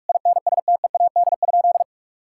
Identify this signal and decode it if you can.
Morse code, heard as a keyed tone, about 46 wpm, 700 Hz. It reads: INSTEAD?
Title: Sentences from Top 1000 Words - Review Part 6 - 45wpm